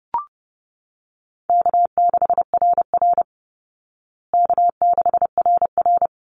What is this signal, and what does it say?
Morse code, heard as a keyed tone, about 30 wpm, 700 Hz. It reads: K6RR K6RR
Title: Easy Call Signs Repeated 2x - 30wpm